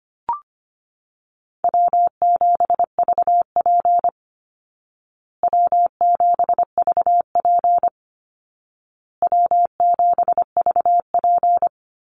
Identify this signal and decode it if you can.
Morse code, heard as a keyed tone, about 25 wpm, 700 Hz. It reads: W7VP W7VP W7VP
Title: Easy Call Signs Repeated 3x - 25wpm